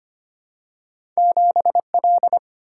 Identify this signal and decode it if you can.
Morse code, heard as a keyed tone, about 25 wpm, 700 Hz. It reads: 7L
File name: Two Letter-Number Combinations - 25wpm